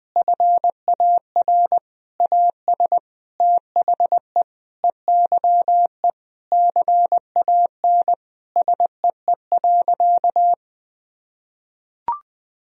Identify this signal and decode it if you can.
Morse code, heard as a keyed tone, about 20 wpm, 700 Hz. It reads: FAR AS THE EYE CAN SEE.
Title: Sentences from Top 1000 Words - Review Part 1 - 20wpm